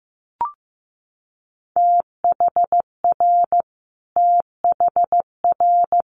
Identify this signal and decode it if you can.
Morse code, heard as a keyed tone, about 15 wpm, 700 Hz. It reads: THR THR
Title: Multi-letter Phonemes Repeated 3x 15wpm